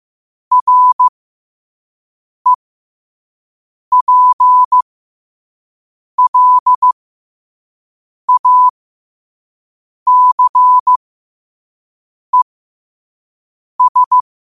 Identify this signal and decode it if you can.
Morse code, heard as a keyed tone, about 15 wpm, spaced as for 5 wpm, 1000 Hz. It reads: REPLACES
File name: qotdfiles / 5wpm